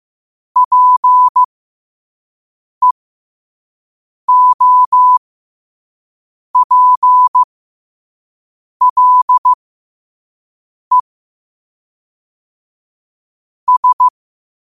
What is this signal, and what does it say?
Morse code, heard as a keyed tone, about 15 wpm, spaced as for 5 wpm, 1000 Hz. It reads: PEOPLE S
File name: qotdfiles / 5wpm